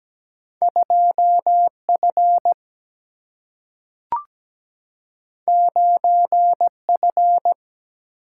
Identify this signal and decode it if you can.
Morse code, heard as a keyed tone, about 17 wpm, 700 Hz. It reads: 2F 9F
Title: Single-Letter Single-Number 17wpm